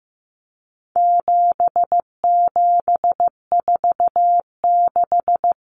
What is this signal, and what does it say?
Morse code, heard as a keyed tone, about 15 wpm, 700 Hz. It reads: 7746
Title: Numbers - 4-Digit 15wpm